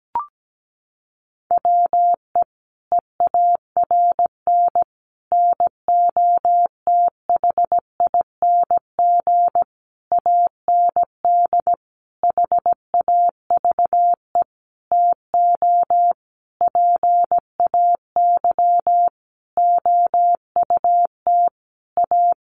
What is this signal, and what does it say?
Morse code, heard as a keyed tone, about 17 wpm, 700 Hz. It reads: WE EARN NOTHING AND HAVE TO PAY OUT A
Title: Sentences from Top 1500 Words 17wpm